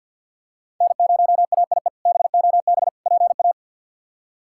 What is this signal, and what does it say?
Morse code, heard as a keyed tone, about 48 wpm, 700 Hz. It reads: N0AIE BOB PA
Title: K1USN SST QSO Practice - 50wpm